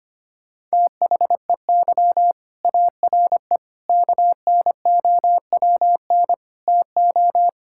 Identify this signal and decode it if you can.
Morse code, heard as a keyed tone, about 25 wpm, 700 Hz. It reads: THEY ARE KNOWN TO
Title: Sentences from Top 1000 Words - Review Part 5 - 25wpm